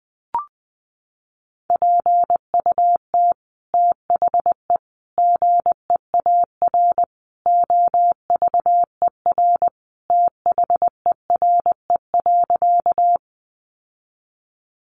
Morse code, heard as a keyed tone, about 20 wpm, 700 Hz. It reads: PUT THE GEAR OVER THERE.
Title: Sentences from Top 1400 Words 20wpm